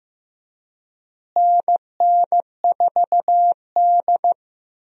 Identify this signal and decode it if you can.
Morse code, heard as a keyed tone, about 16 wpm, 700 Hz. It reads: NN4D
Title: USA Call Signs - 4 Characters - Repeated 3x - 15wpm